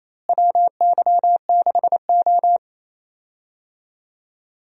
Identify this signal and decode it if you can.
Morse code, heard as a keyed tone, about 28 wpm, 700 Hz. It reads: WY6O
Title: USA Call Signs - 4 Characters - Repeated 2x - 28wpm